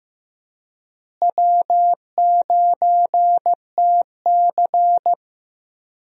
Morse code, heard as a keyed tone, about 15 wpm, 700 Hz. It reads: W9TC
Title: USA Call Signs - 4 Characters - Repeated 2x - 15wpm